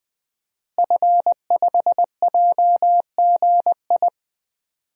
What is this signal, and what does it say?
Morse code, heard as a keyed tone, about 21 wpm, 700 Hz. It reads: F5JGI
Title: DXCC-100 Call Signs - 5 Characters - Repeated 2x - 20wpm